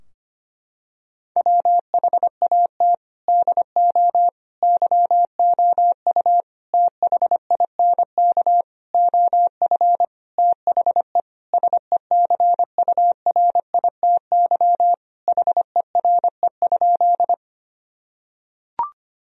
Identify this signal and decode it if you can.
Morse code, heard as a keyed tone, about 25 wpm, 700 Hz. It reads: WHAT DO YOU THINK OF THE SECURITY HERE?
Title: Sentences from Top 1000 Words - Review Part 4 - 25wpm